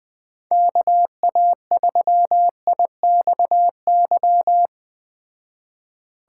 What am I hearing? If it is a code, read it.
Morse code, heard as a keyed tone, about 21 wpm, 700 Hz. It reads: KA3IXY